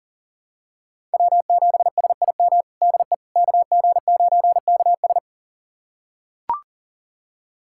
Morse code, heard as a keyed tone, about 40 wpm, 700 Hz. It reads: W7SIM DE KG9KS
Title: USA Callsign DE Callsign - 5 Characters - 40wpm